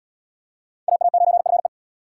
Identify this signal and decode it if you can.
Morse code, heard as a keyed tone, about 68 wpm, 700 Hz. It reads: AA9FE